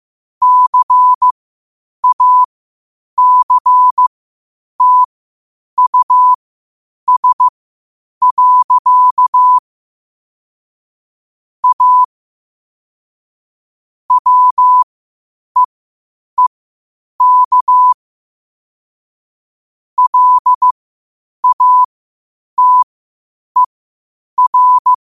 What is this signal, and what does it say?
Morse code, heard as a keyed tone, about 15 wpm, spaced as for 8 wpm, 1000 Hz. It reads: CACTUS. A WEEK LATER
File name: qotdfiles / 8wpm